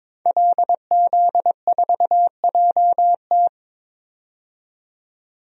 Morse code, heard as a keyed tone, about 21 wpm, 700 Hz. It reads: LZ4JT
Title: DXCC-100 Call Signs - 5 Characters - 22wpm